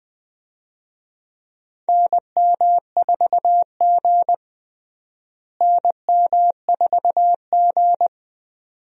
Morse code, heard as a keyed tone, about 21 wpm, 700 Hz. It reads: NM4G NM4G
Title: USA Call Signs - 4 Characters - Repeated 2x - 20wpm